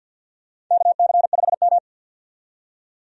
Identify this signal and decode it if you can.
Morse code, heard as a keyed tone, about 45 wpm, 700 Hz. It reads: KC5M
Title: USA Call Signs - 4 Characters - 50wpm